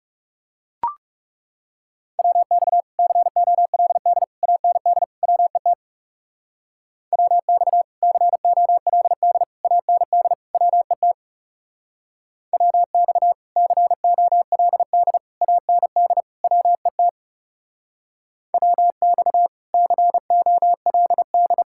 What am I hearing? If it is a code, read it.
Morse code, heard as a keyed tone, about 36 wpm, 700 Hz. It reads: WX COLD AND WET WX COLD AND WET WX COLD AND WET WX COLD
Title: Speed-Racing Common QSO Phrases - 30wpm